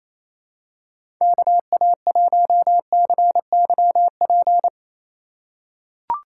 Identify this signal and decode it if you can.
Morse code, heard as a keyed tone, about 28 wpm, 700 Hz. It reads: KA1CYP
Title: USA Call Signs - 6 Characters - 3x Character Spacing - 28wpm